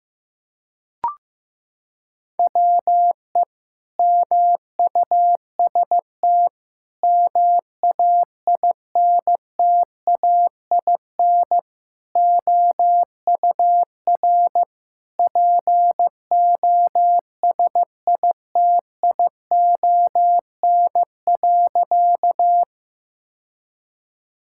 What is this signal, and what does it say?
Morse code, heard as a keyed tone, about 15 wpm, 700 Hz. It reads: WE MUST MAINTAIN OUR POSITION.
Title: Sentences from Top 1000 Words - Review Part 5 - 15wpm